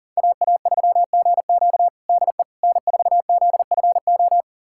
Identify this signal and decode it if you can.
Morse code, heard as a keyed tone, about 40 wpm, 700 Hz. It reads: AA3GQ DE N4ZFO
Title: USA Callsign DE Callsign - 5 Characters - 40wpm